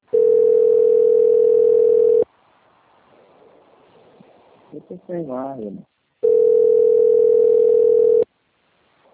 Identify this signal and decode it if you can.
Ringback tone (North America), 440 + 480 Hz, with speech.
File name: out- - - - - .3284